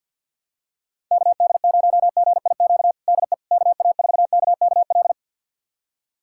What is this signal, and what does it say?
Morse code, heard as a keyed tone, about 51 wpm, 700 Hz. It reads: KD0GIQ DE KA4KKL
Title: USA Callsign DE Callsign - 6 Characters - 50wpm